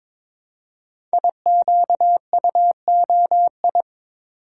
Morse code, heard as a keyed tone, about 22 wpm, 700 Hz. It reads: IQUOI
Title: Code Groups - 5 Letters - 22wpm